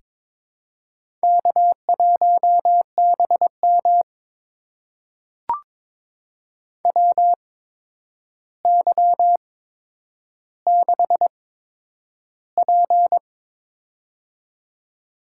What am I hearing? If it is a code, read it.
Morse code, heard as a keyed tone, about 22 wpm, spaced as for 9 wpm, 700 Hz. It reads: K1BM WY6P
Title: USA Call Signs - 8x Character Spacing - 22wpm